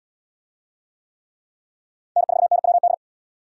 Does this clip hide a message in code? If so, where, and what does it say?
Yes: Morse code from 2.2 to 2.9 s, heard as a keyed tone, about 71 wpm, 700 Hz. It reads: N4NWD